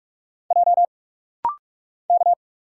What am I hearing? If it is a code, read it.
Morse code, heard as a keyed tone, about 46 wpm, 700 Hz. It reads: J K
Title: Single Letters - Rapid-Fire 45wpm